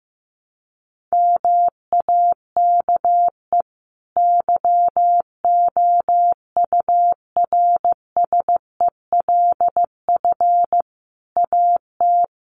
Morse code, heard as a keyed tone, about 15 wpm, 700 Hz. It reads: MAKE YOURSELF AT
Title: Sentences from Top 600 Words 15wpm